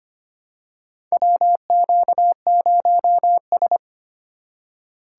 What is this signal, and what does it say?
Morse code, heard as a keyed tone, about 25 wpm, 700 Hz. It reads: WQ0S
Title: USA Call Signs - 4 Characters - 6x Character Spacing - 25wpm